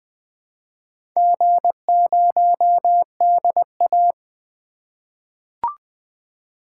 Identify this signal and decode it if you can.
Morse code, heard as a keyed tone, about 20 wpm, 700 Hz. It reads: G0DA
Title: Four Letter-Number Combinations - 20wpm